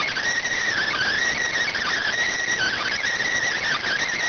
SSTV (slow-scan television), PD 290.